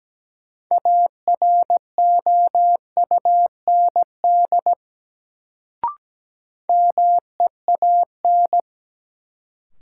Morse code, heard as a keyed tone, about 17 wpm, 700 Hz. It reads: AROUND MEAN
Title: Sets of 2 Words - New Word Drill - 17wpm